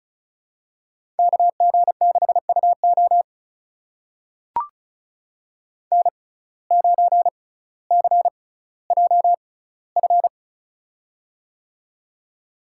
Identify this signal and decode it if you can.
Morse code, heard as a keyed tone, about 36 wpm, spaced as for 18 wpm, 700 Hz. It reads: KG6UO N9CJF
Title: USA Call Signs - 5 Characters - 6x Character Spacing - 35wpm